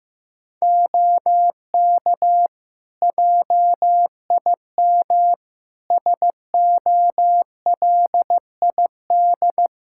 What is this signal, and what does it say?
Morse code, heard as a keyed tone, about 15 wpm, 700 Hz. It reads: OK JIM SOLID
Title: Practice QSO - Exchange 3 - More Info - 15wpm